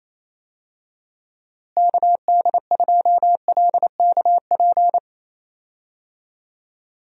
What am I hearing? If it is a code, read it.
Morse code, heard as a keyed tone, about 27 wpm, 700 Hz. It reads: KD2LKP
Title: USA Call Signs - 6 Characters - Repeated 3x - 28wpm